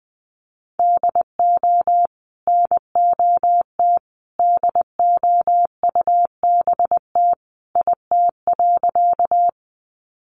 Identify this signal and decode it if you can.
Morse code, heard as a keyed tone, about 20 wpm, 700 Hz. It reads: DO NOT DOUBT IT.
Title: Sentences from Top 1500 Words 20wpm